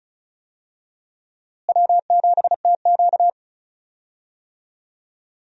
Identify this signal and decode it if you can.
Morse code, heard as a keyed tone, about 35 wpm, 700 Hz. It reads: W7TQ
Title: USA Call Signs - 4 Characters - Repeated 3x - 35wpm